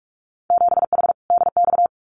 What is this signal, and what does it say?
Morse code, heard as a keyed tone, about 46 wpm, 700 Hz. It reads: 7H DX
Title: Winter Field Day Exchanges Encore 45wpm